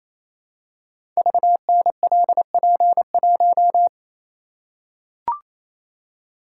Morse code, heard as a keyed tone, about 27 wpm, 700 Hz. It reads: VNLP1
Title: Code Groups - 5 Letter:Numbers - 28wpm